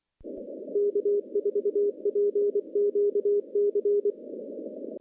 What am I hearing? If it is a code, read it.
Morse code, heard as a keyed tone, about 24 wpm, 410 Hz. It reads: K4PQC
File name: contests / CW-SS-k5tr- -0400